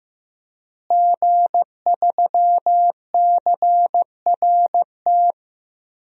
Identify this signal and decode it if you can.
Morse code, heard as a keyed tone, about 15 wpm, 700 Hz. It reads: G3CRT
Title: DXCC-100 Call Signs - 5 Characters - Repeated 2x - 15wpm